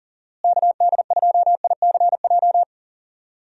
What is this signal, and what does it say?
Morse code, heard as a keyed tone, about 40 wpm, 700 Hz. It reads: KD2ICJ
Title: USA Call Signs - 6 Characters - Repeated 3x - 40wpm